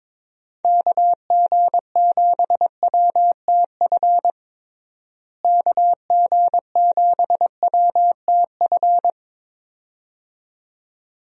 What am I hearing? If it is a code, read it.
Morse code, heard as a keyed tone, about 22 wpm, 700 Hz. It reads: KG7WTF KG7WTF
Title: USA Call Signs - 6 Characters - Repeated 2x - 22wpm